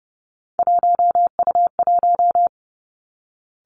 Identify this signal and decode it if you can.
Morse code, heard as a keyed tone, about 30 wpm, 700 Hz. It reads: 1U1
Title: Three Challenging Letter-Number Combinations 30wpm